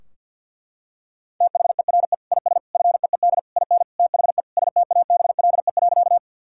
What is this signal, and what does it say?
Morse code, heard as a keyed tone, about 50 wpm, 700 Hz. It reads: THERE IS FEED IN THE STABLE.
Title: Sentences from Top 1600 Words 50wpm